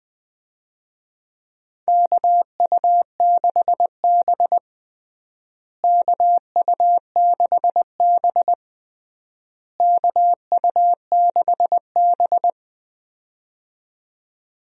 Morse code, heard as a keyed tone, about 19 wpm, 700 Hz. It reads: KU6B KU6B KU6B
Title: USA Call Signs - 4 Characters - Repeated 3x - 20wpm